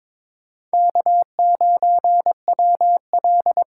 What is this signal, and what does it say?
Morse code, heard as a keyed tone, about 26 wpm, 700 Hz. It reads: K9WL
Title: Speed Racing - USA Call Signs 22wpm